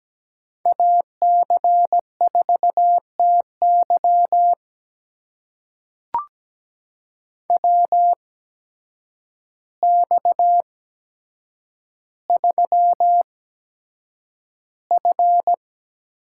Morse code, heard as a keyed tone, about 17 wpm, spaced as for 8 wpm, 700 Hz. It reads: AC4TY WX3F
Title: USA Call Signs - 8x Character Spacing - 17wpm